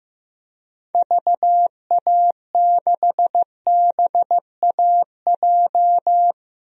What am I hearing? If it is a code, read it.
Morse code, heard as a keyed tone, about 15 wpm, 700 Hz. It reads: VA6BAJ
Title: DXCC-100 Call Signs - 6 Characters - Repeated 3x - 15wpm